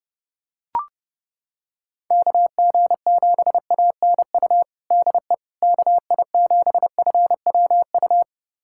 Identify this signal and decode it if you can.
Morse code, heard as a keyed tone, about 32 wpm, 700 Hz. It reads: KG7ANU DE KI7FWU
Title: USA Callsign DE Callsign - 6 Characters - 30wpm